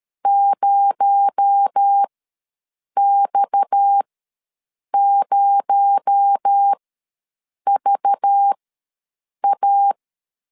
Morse code, heard as a keyed tone, about 13 wpm, spaced as for 7 wpm, 800 Hz. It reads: 0X0VA
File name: LEZIONI / 6 PKX0 / 6 PKX0 ws 12wpm 3448